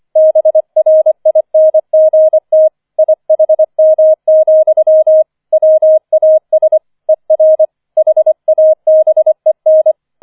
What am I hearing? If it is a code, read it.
Morse code, heard as a keyed tone, about 25 wpm, 630 Hz. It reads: BRINGT IHM, WAS ER HABEN
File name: 005 Fünfter Streich 25wpm